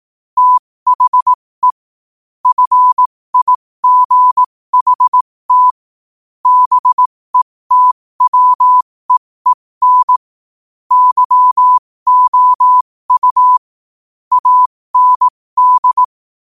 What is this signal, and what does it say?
Morse code, heard as a keyed tone, about 18 wpm, spaced as for 15 wpm, 1000 Hz. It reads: THE FIGHT BETWEEN YOU AND
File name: qotdfiles / 15wpm